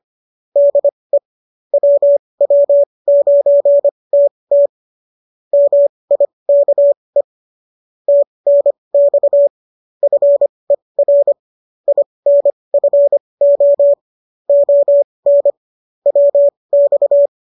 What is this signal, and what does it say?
Morse code, heard as a keyed tone, about 25 wpm, spaced as for 20 wpm, 570 Hz. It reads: DE WW9TT MIKE TNX FER INFO ON WX